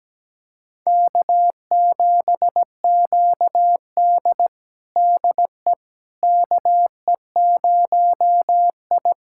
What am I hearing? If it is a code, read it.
Morse code, heard as a keyed tone, about 17 wpm, 700 Hz. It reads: K7QD DE KE0I
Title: USA Callsign DE Callsign - 4 Characters - 17wpm